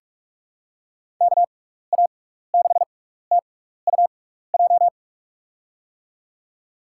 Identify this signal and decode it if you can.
Morse code, heard as a keyed tone, about 44 wpm, spaced as for 15 wpm, 700 Hz. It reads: KA6TUJ
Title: USA Call Signs - 6 Characters - 6x Character Spacing - 45wpm